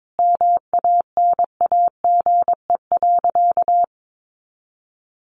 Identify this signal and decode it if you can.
Morse code, heard as a keyed tone, about 22 wpm, 700 Hz. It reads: MANAGE.
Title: Sentences from Top 1400 Words 22wpm